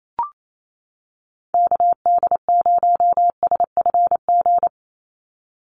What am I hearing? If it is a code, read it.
Morse code, heard as a keyed tone, about 28 wpm, 700 Hz. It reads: KD0SFG
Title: USA Call Signs 28wpm